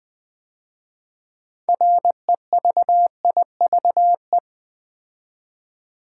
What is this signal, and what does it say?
Morse code, heard as a keyed tone, about 20 wpm, 700 Hz. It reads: REVIVE
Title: 6 Letter Words - First 3 Letters Given - 20wpm